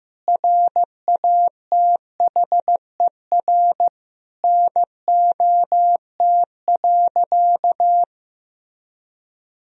Morse code, heard as a keyed tone, about 16 wpm, 700 Hz. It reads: RATHER NOT.
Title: Sentences from Top 1000 Words - Review Part 4 - 15wpm